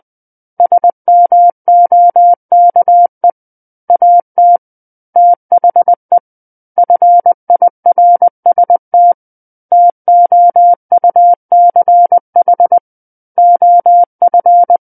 Morse code, heard as a keyed tone, about 20 wpm, 700 Hz. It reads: SMOKE AT THE FIRST TOUCH OF
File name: 11-Chapter 11 - 20 WPM